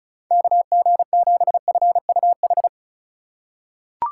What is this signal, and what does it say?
Morse code, heard as a keyed tone, about 36 wpm, 700 Hz. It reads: KG7FUH
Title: USA Call Signs - 6 Characters - 3x Character Spacing - 35wpm